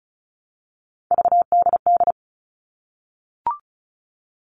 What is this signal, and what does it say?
Morse code, heard as a keyed tone, about 34 wpm, 700 Hz. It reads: VDD
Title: All Cut Numbers - 3-Digit 35wpm